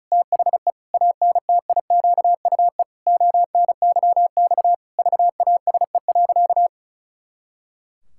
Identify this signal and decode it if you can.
Morse code, heard as a keyed tone, about 36 wpm, 700 Hz. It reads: THE ANTIQUE ONYX VASE.
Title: Pangram Challenge v2 - 35wpm